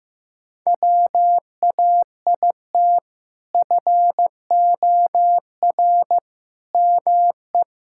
Morse code, heard as a keyed tone, about 15 wpm, 700 Hz. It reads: WAIT FOR ME
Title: Sentences from Top 1000 Words - Review Part 6 - 15wpm